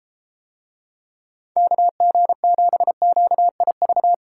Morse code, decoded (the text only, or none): KG7QIV